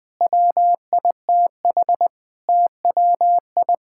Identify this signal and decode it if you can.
Morse code, heard as a keyed tone, about 21 wpm, 700 Hz. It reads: WITH TWI
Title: Difficult Sentences with Hints - 20wpm